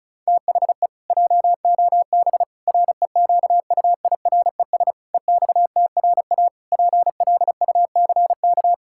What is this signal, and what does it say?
Morse code, heard as a keyed tone, about 36 wpm, 700 Hz. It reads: THE JOB REQUIRES EXTRA PLUCK
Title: Pangram Challenge v2 - 35wpm